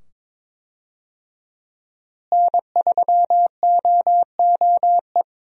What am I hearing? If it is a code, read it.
Morse code, heard as a keyed tone, about 21 wpm, 700 Hz. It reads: N3OOE
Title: USA Call Signs - 5 Characters - Repeated 2x - 22wpm